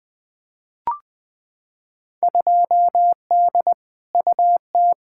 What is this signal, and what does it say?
Morse code, heard as a keyed tone, about 20 wpm, 700 Hz. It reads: 2D UT
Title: Field Day Exchanges Encore - 20wpm